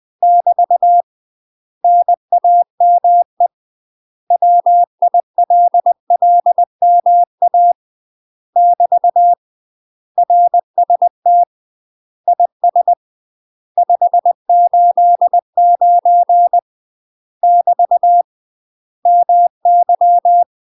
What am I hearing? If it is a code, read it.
Morse code, heard as a keyed tone, about 20 wpm, 700 Hz. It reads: <BT> NAME WILLMA <BT> RST IS 589 <BT> MY
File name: UK MASTER QSOs TRAINING / 20WPM / UK MASTER QSOs 0023